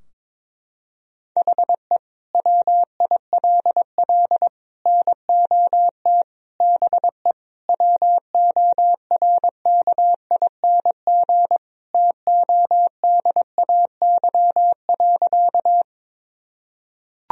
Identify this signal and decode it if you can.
Morse code, heard as a keyed tone, about 22 wpm, 700 Hz. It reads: HE WILL NOT BE WORKING TODAY.
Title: Sentences from Top 1000 Words - Review Part 4 - 22wpm